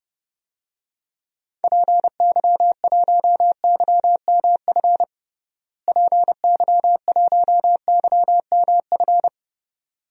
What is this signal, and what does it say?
Morse code, heard as a keyed tone, about 30 wpm, 700 Hz. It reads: PY1YMF PY1YMF
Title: DXCC-100 Call Signs - 6 Characters - Repeated 2x - 30wpm